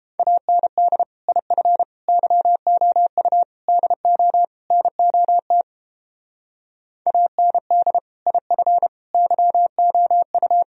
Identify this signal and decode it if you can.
Morse code, heard as a keyed tone, about 29 wpm, 700 Hz. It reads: AND IF YOU DO NOT AND IF YOU